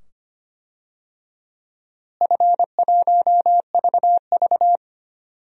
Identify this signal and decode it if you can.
Morse code, heard as a keyed tone, about 25 wpm, 700 Hz. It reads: F1VV
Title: DXCC-100 Call Signs - 4 Characters - Repeated 2x - 25wpm